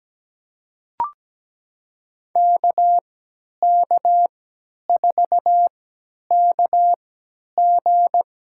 Morse code, heard as a keyed tone, about 17 wpm, spaced as for 10 wpm, 700 Hz. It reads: KK4KG
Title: USA Call Signs - 3x Character Spacing - 17wpm